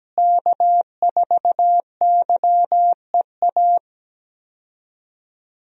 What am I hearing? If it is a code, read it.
Morse code, heard as a keyed tone, about 17 wpm, 700 Hz. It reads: K4YEA
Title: USA Call Signs - 5 Characters - Repeated 3x - 17wpm